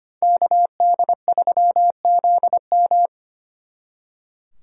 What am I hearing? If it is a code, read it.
Morse code, heard as a keyed tone, about 33 wpm, 700 Hz. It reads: KD3ZM